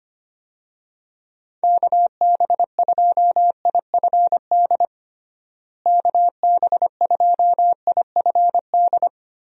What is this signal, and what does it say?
Morse code, heard as a keyed tone, about 25 wpm, 700 Hz. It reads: KB2IFD KB2IFD
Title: USA Call Signs - 6 Characters - Repeated 2x - 25wpm